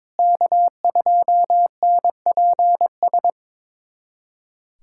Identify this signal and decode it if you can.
Morse code, heard as a keyed tone, about 22 wpm, 700 Hz. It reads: K2NPS
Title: USA Callsign DE Callsign 22wpm